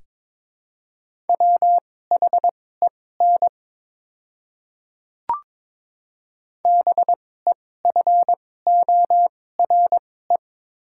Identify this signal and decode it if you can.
Morse code, heard as a keyed tone, about 22 wpm, spaced as for 16 wpm, 700 Hz. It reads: WHEN BEFORE
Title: 2x Character Spacing - Top 200 Words - 22wpm